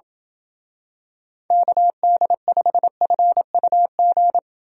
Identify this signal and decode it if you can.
Morse code, heard as a keyed tone, about 27 wpm, 700 Hz. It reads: KD5FUG